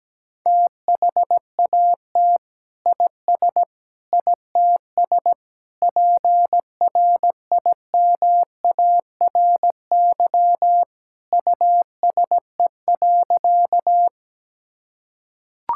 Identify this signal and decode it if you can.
Morse code, heard as a keyed tone, about 17 wpm, 700 Hz. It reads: THAT IS ITS PRIMARY USE.
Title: Sentences from Top 1000 Words - Review Part 5 - 17wpm